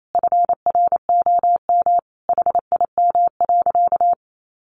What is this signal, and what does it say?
Morse code, heard as a keyed tone, about 28 wpm, 700 Hz. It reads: FROM HIM.
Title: Sentences from Top 1500 Words 28wpm